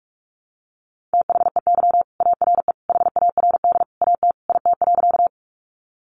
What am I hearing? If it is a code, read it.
Morse code, heard as a keyed tone, about 48 wpm, 700 Hz. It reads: THEY ARE HARD AT IT.
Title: Sentences from Top 300 Words 45wpm